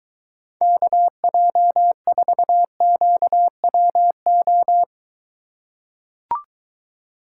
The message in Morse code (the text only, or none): KJ4QWO